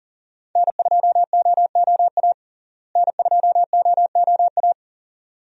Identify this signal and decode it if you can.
Morse code, heard as a keyed tone, about 40 wpm, 700 Hz. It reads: N2OOA N2OOA
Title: USA Call Signs - 5 Characters - Repeated 2x - 40wpm